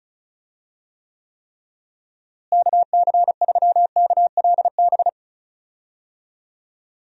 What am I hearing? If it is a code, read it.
Morse code, heard as a keyed tone, about 34 wpm, 700 Hz. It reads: KC3KLB
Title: USA Call Signs - 6 Characters - Repeated 3x - 35wpm